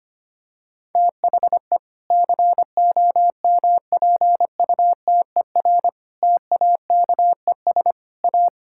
Morse code, heard as a keyed tone, about 25 wpm, 700 Hz. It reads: THE COMPUTER TAKES A
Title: Sentences from Top 1000 Words - Review Part 6 - 25wpm